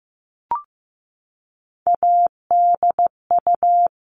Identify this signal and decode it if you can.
Morse code, heard as a keyed tone, about 15 wpm, 700 Hz. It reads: ADU
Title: All Cut Numbers - 3-Digit 15wpm